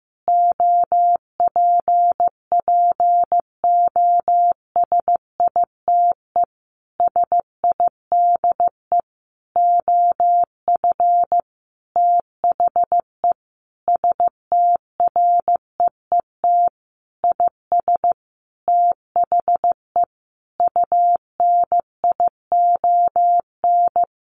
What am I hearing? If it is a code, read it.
Morse code, heard as a keyed tone, about 15 wpm, 700 Hz. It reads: OPPOSITE SIDE OF THE STREET IS THE UNION
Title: Sentences from Top 1400 Words 15wpm